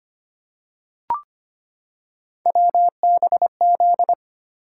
Morse code, heard as a keyed tone, about 25 wpm, 700 Hz. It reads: WBZ
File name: Three Letter-Number Combinations - 25wpm